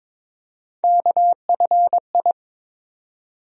Morse code, heard as a keyed tone, about 21 wpm, 700 Hz. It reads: KFI
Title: USA Call Sign Postfixes - 3 Letters 22wpm